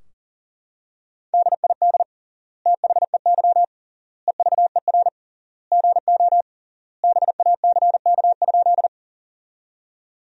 Morse code, heard as a keyed tone, about 40 wpm, 700 Hz. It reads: DID THEY EVER GO BACK?
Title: 3x Word-Spacing - Sentences from Top 400 Words - 40wpm